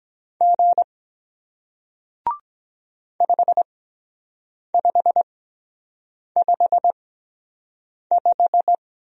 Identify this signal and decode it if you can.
Morse code, heard as a keyed tone, about 22 wpm, spaced as for 7 wpm, 700 Hz. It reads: G 5555